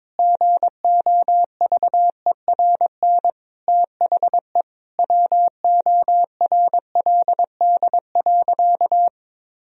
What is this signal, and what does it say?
Morse code, heard as a keyed tone, about 22 wpm, 700 Hz. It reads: GOVERN THE WORLD.
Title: Sentences from Top 1000 Words - Review Part 2 - 22wpm